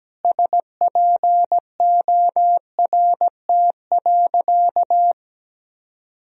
Morse code, heard as a keyed tone, about 17 wpm, 700 Hz. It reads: SPORT.
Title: Sentences from Top 1000 Words - Review Part 4 - 17wpm